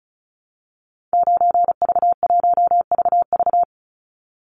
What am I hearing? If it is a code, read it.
Morse code, heard as a keyed tone, about 34 wpm, 700 Hz. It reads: 9V1VV
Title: Call Signs Repeated 2x - 35wpm